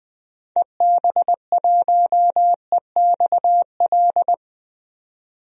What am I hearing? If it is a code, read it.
Morse code, heard as a keyed tone, about 20 wpm, 700 Hz. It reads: EB1EXL
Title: DXCC-100 Call Signs - 6 Characters - 20wpm